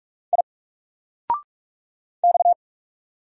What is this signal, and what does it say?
Morse code, heard as a keyed tone, about 44 wpm, 700 Hz. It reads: I X
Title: Single Letter-Number Alt 45wpm